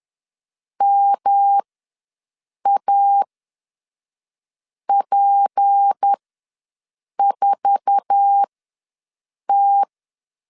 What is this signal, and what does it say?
Morse code, heard as a keyed tone, about 10 wpm, spaced as for 5 wpm, 800 Hz. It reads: MAP4T
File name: LEZIONI / 10 SIGNS / 10B INT ws 10wpm 4134